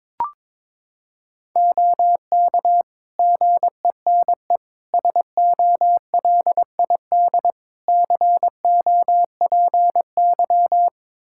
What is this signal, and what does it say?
Morse code, heard as a keyed tone, about 21 wpm, 700 Hz. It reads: OK GENE SOLID COPY
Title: Practice QSO - Exchange 3 - More Info - 22wpm